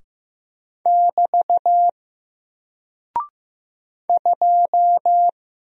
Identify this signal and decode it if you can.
Morse code, heard as a keyed tone, about 15 wpm, 700 Hz. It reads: <BT> 2
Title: Single Character (Letter:Number:Punctuation:Prosign) 15wpm